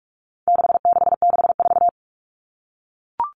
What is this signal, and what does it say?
Morse code, heard as a keyed tone, about 44 wpm, 700 Hz. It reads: 6664
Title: Numbers - 4-Digit 45wpm